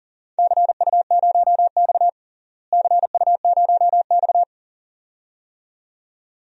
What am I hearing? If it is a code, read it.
Morse code, heard as a keyed tone, about 40 wpm, 700 Hz. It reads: CU0X CU0X
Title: DXCC-100 Call Signs - 4 Characters - Repeated 2x - 40wpm